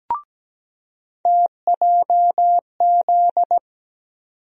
Morse code, heard as a keyed tone, about 17 wpm, 700 Hz. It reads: TJZ
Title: USA Call Sign Postfixes - 3 Letters 17wpm